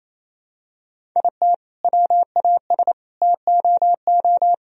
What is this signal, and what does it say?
Morse code, heard as a keyed tone, about 28 wpm, 700 Hz. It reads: IT WAS TOO
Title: Sentences from Top 1000 Words - Review Part 1 - 28wpm